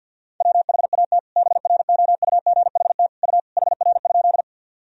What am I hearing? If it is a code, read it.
Morse code, heard as a keyed tone, about 50 wpm, 700 Hz. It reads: WHAT BROUGHT U HR?
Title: Sentences from Top 500 Words and Abbreviations 50wpm